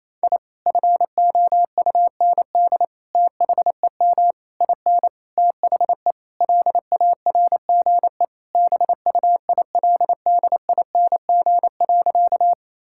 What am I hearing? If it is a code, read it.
Morse code, heard as a keyed tone, about 28 wpm, 700 Hz. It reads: I FOUND THEM IN THE LARGE BUILDING.
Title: Sentences from Top 1000 Words - Review Part 5 - 28wpm